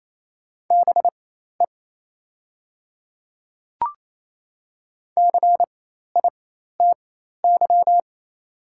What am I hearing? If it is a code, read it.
Morse code, heard as a keyed tone, about 28 wpm, spaced as for 13 wpm, 700 Hz. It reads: BE CITY
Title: 4x Character Spacing - Top 300 Words - 28wpm